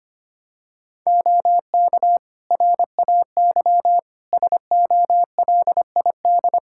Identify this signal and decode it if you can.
Morse code, heard as a keyed tone, about 25 wpm, 700 Hz. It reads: OK RAY SOLID
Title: Practice QSO - Exchange 3 - More Info - 25wpm